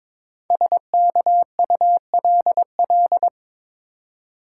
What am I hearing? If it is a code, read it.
Morse code, heard as a keyed tone, about 23 wpm, 700 Hz. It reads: SKULL
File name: 5 Letter Words - First 3 Letters Given - 22wpm